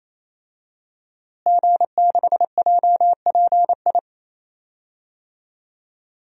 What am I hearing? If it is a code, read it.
Morse code, heard as a keyed tone, about 28 wpm, 700 Hz. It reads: G6JPI
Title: DXCC-100 Call Signs - 5 Characters - Repeated 3x - 28wpm